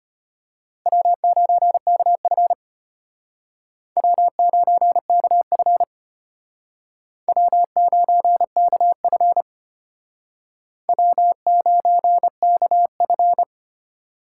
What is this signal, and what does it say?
Morse code, heard as a keyed tone, about 34 wpm, 700 Hz. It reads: W9KF W9KF W9KF W9KF